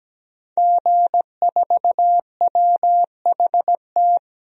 Morse code, heard as a keyed tone, about 17 wpm, 700 Hz. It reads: G4WHT
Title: DXCC-100 Call Signs - 5 Characters - 17wpm